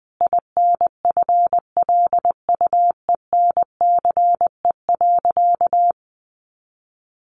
Morse code, heard as a keyed tone, about 20 wpm, 700 Hz. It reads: INFLUENCE.
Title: Sentences from Top 1100 Words 20wpm